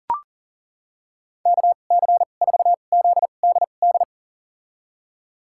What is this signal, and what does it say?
Morse code, heard as a keyed tone, about 40 wpm, spaced as for 29 wpm, 700 Hz. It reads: KC4ZDD
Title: USA Call Signs - 6 Characters - 2x Character Spacing - 40wpm